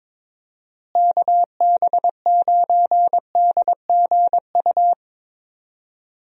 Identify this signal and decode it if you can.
Morse code, heard as a keyed tone, about 22 wpm, 700 Hz. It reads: KB9DGU
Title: USA Call Signs - 6 Characters - Repeated 3x - 22wpm